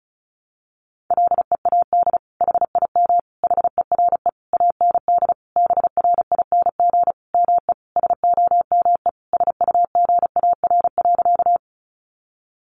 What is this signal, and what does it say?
Morse code, heard as a keyed tone, about 34 wpm, 700 Hz. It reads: LEAD HIM HERE AND BRING ME SOME SUGAR.
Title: Sentences from Top 1200 Words 35wpm